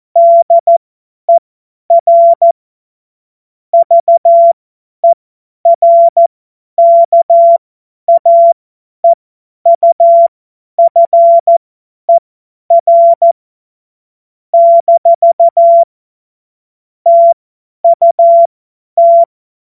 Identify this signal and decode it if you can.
Morse code, heard as a keyed tone, about 14 wpm, spaced as for 10 wpm, 680 Hz. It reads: DER VERKAEUFER - TUT
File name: Morsekurs / Witze / Witz 09 April 2024 50 F70